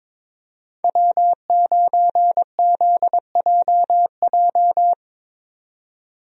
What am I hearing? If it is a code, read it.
Morse code, heard as a keyed tone, about 22 wpm, 700 Hz. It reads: W9ZJJ
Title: USA Call Signs - 5 Characters - 22wpm